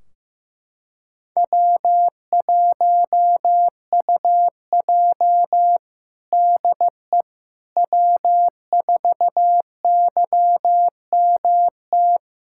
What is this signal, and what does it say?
Morse code, heard as a keyed tone, about 15 wpm, 700 Hz. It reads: W1UJ DE W4YMT
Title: Practice QSO - Exchange 4 - Ending - 15wpm